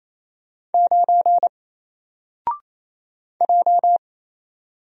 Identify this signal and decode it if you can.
Morse code, heard as a keyed tone, about 27 wpm, 700 Hz. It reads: 9 J
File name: Single-Letter-Number-28wpm